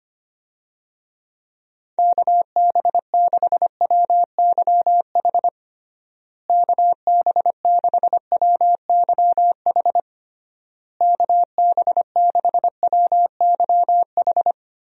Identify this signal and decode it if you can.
Morse code, heard as a keyed tone, about 25 wpm, 700 Hz. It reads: KB6WYH KB6WYH KB6WYH
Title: USA Call Signs - 6 Characters - Repeated 3x - 25wpm